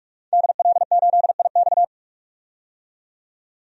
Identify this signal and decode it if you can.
Morse code, heard as a keyed tone, about 42 wpm, 700 Hz. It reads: DL8IX
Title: DXCC-100 Call Signs - 5 Characters - 45wpm